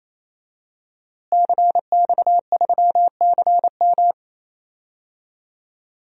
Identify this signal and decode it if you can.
Morse code, heard as a keyed tone, about 28 wpm, 700 Hz. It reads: CX3CM
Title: DXCC-100 Call Signs - 5 Characters - Repeated 3x - 28wpm